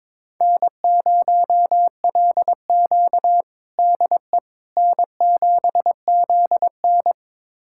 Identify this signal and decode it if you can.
Morse code, heard as a keyed tone, about 21 wpm, 700 Hz. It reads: N0LQ DE N7ZN
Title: USA Callsign DE Callsign - 4 Characters - 22wpm